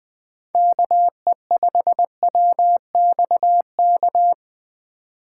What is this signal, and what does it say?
Morse code, heard as a keyed tone, about 20 wpm, 700 Hz. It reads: KE5WXK
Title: USA Call Signs - 6 Characters - 2x Character Spacing - 20wpm